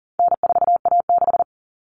Morse code, heard as a keyed tone, about 40 wpm, 700 Hz. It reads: N4A6
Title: All Cut Numbers - 4-Digit 40wpm